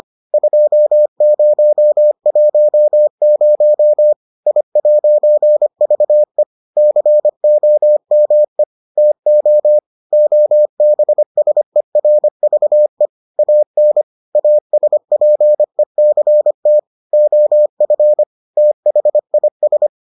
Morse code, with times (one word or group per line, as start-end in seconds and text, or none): 0.3-4.1 2010
4.5-6.4 I'VE
6.8-8.6 COME
9.0-9.8 TO
10.1-13.1 OBSERVE
13.4-14.0 AN
14.3-16.8 ASPECT
17.1-18.2 OF
18.6-19.9 THIS